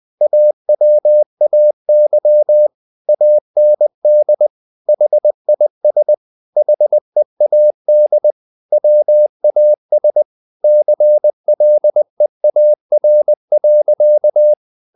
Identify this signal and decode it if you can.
Morse code, heard as a keyed tone, about 20 wpm, 600 Hz. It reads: AWAY AND HIS HEAD WAS CLEAR.
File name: the old man and the sea part 2 20wpm